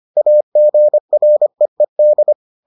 Morse code, heard as a keyed tone, about 25 wpm, 600 Hz. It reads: AGREED